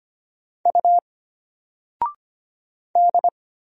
Morse code, heard as a keyed tone, about 25 wpm, 700 Hz. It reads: U D
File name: Single-Letter-Number-25wpm